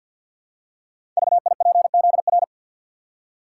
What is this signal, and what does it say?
Morse code, heard as a keyed tone, about 52 wpm, 700 Hz. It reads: VIPZR
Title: Code Groups - 5 Letters - 50wpm